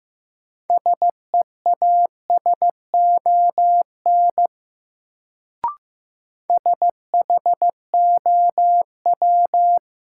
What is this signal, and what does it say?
Morse code, heard as a keyed tone, about 15 wpm, 700 Hz. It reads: SEASON SHOW
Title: Sets of 5 Words - New Word Drill - 15wpm